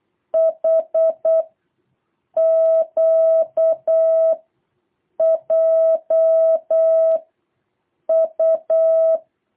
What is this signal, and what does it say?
Morse code, heard as a keyed tone, about 8 wpm, 660 Hz. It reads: H Q J U
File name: jh8cbh / 2023cw